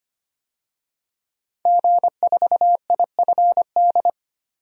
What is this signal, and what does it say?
Morse code, heard as a keyed tone, about 25 wpm, 700 Hz. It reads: G4IFD
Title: DXCC-100 Call Signs - 5 Characters - 25wpm